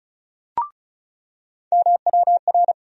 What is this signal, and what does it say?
Morse code, heard as a keyed tone, about 34 wpm, 700 Hz. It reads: MWR
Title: USA Call Sign Postfixes - 3 Letters 35wpm